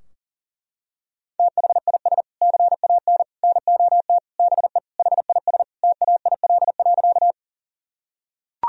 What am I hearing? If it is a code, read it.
Morse code, heard as a keyed tone, about 40 wpm, 700 Hz. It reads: THIS CAN NOT BE HIS TAIL.
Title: Sentences from Top 1000 Words - Review Part 3 - 40wpm